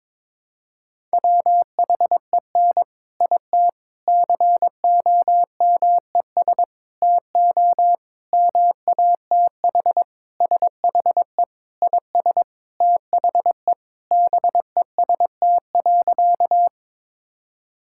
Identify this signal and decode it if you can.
Morse code, heard as a keyed tone, about 22 wpm, 700 Hz. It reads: WHEN IT COMES TO MATH SHE IS THE BEST.
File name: practice / 22wpm / Sentences with 3-Word Phrases - 22wpm